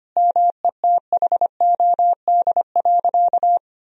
Morse code, heard as a keyed tone, about 25 wpm, 700 Hz. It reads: METHOD.
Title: Sentences from Top 1000 Words - Review Part 3 - 25wpm